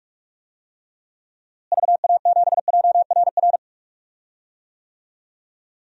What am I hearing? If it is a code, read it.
Morse code, heard as a keyed tone, about 46 wpm, 700 Hz. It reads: VA7JRR
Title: DXCC-100 Call Signs - 6 Characters - Repeated 3x - 45wpm